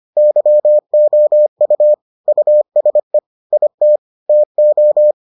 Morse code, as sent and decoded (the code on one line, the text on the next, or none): -.-- --- ..- / ..- ... / .. - / - ---
YOU USE IT TO